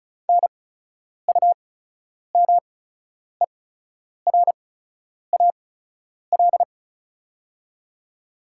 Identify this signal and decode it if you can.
Morse code, heard as a keyed tone, about 36 wpm, spaced as for 10 wpm, 700 Hz. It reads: NUMERAL
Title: 8x Character Spacing - Top 500 Words - 35wpm